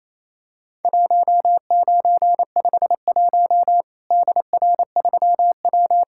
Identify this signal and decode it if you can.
Morse code, heard as a keyed tone, about 28 wpm, 700 Hz. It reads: 1951 DR3W
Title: Worked All Europe DX Contest - QTC Exchange - 28wpm